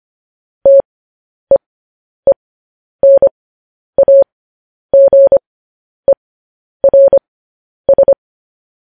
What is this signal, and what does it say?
Morse code, heard as a keyed tone, about 25 wpm, spaced as for 10 wpm, 550 Hz. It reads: TEENAGERS
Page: news headlines in morse code at 10 wpm